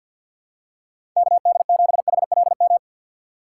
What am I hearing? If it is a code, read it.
Morse code, heard as a keyed tone, about 53 wpm, 700 Hz. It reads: KD7HLM
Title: USA Call Signs - 6 Characters - 3x Character Spacing - 50wpm